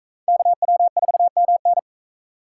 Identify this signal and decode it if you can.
Morse code, heard as a keyed tone, about 41 wpm, 700 Hz. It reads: KW4MN